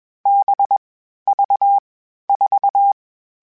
Morse code, heard as a keyed tone, about 21 wpm, 800 Hz. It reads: B V 4